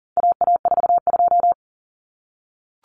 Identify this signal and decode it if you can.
Morse code, heard as a keyed tone, about 40 wpm, 700 Hz. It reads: AA42
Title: Common Cut Numbers - 4-Digit 40wpm